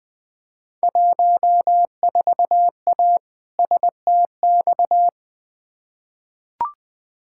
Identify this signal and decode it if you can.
Morse code, heard as a keyed tone, about 20 wpm, 700 Hz. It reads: 14A STX
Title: Field Day Exchanges Encore - 20wpm